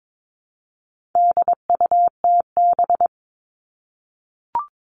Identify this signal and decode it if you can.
Morse code, heard as a keyed tone, about 22 wpm, 700 Hz. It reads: DUTB
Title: All Cut Numbers - 4-Digit 22wpm